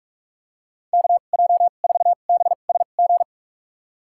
Morse code, heard as a keyed tone, about 46 wpm, spaced as for 33 wpm, 700 Hz. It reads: KJ4BSG
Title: USA Call Signs - 6 Characters - 2x Character Spacing - 45wpm